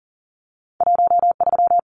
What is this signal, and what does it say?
Morse code, heard as a keyed tone, about 40 wpm, 700 Hz. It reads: 13
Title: Numbers - 2-Digit 40wpm